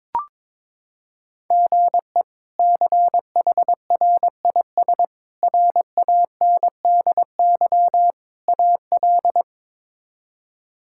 Morse code, heard as a keyed tone, about 22 wpm, 700 Hz. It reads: GE CHRIS RANDY AL
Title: K1USN SST QSO Practice - 22wpm